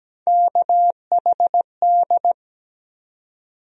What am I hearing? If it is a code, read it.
Morse code, heard as a keyed tone, about 17 wpm, 700 Hz. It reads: KHD
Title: USA Call Sign Postfixes - 3 Letters 17wpm